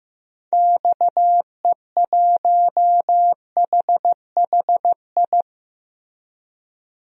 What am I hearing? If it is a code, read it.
Morse code, heard as a keyed tone, about 15 wpm, 700 Hz. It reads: XE1HHI
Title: DXCC-100 Call Signs - 6 Characters - Repeated 3x - 15wpm